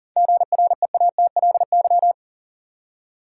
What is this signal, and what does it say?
Morse code, heard as a keyed tone, about 40 wpm, 700 Hz. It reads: GREATLY
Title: Seven Letter Words - 40wpm